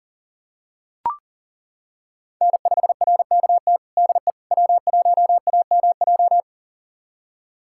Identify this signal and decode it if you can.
Morse code, heard as a keyed tone, about 40 wpm, 700 Hz. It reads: N5RKT DE W1AMJ
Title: USA Callsign DE Callsign - 5 Characters - 40wpm